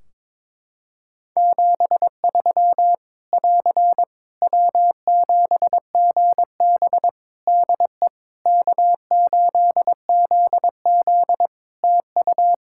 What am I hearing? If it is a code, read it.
Morse code, heard as a keyed tone, about 22 wpm, 700 Hz. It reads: 73 <AR> W7GB DE K8ZZ TU
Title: Practice QSO - Exchange 4 - Ending - 22wpm